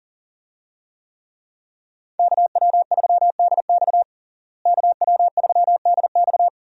Morse code, heard as a keyed tone, about 40 wpm, 700 Hz. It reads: KW3DX KW3DX
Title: USA Call Signs - 5 Characters - Repeated 3x - 40wpm